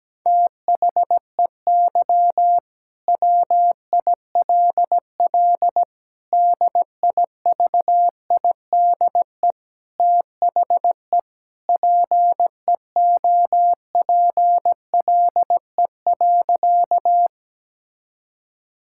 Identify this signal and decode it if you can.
Morse code, heard as a keyed tone, about 17 wpm, 700 Hz. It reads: THEY WILL DIVIDE THE PEOPLE.
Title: Sentences from Top 1600 Words 17wpm